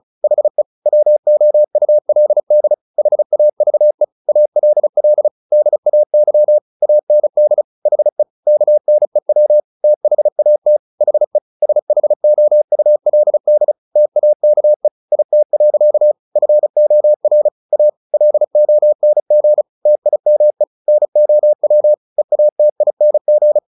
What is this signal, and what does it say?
Morse code, heard as a keyed tone, about 34 wpm, 600 Hz. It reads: HE WOULD HAVE ALL DAY AND HE KNEW THAT HE SHOULD TAKE IT. FOR A LONG TIME NOW EATING
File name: the old man and the sea / the old man and the sea 1 35wpm